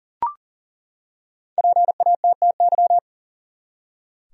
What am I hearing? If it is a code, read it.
Morse code, heard as a keyed tone, about 40 wpm, 700 Hz. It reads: PATTY
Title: USA Common Names - 40wpm